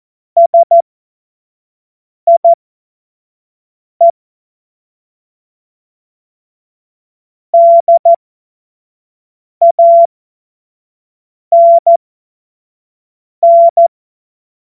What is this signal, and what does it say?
Morse code, heard as a keyed tone, about 14 wpm, spaced as for 5 wpm, 680 Hz. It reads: SIE DANN
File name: Witz-15-Juli-2025-25-F70